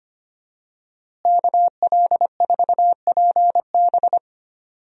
Morse code, heard as a keyed tone, about 25 wpm, 700 Hz. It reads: KL4PB
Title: USA Call Signs - 5 Characters - Repeated 3x - 25wpm